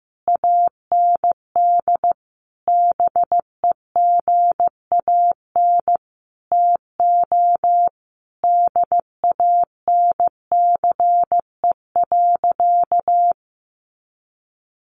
Morse code, heard as a keyed tone, about 15 wpm, 700 Hz. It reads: AND BEGAN TO DANCE.
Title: Sentences from Top 900 Words 15wpm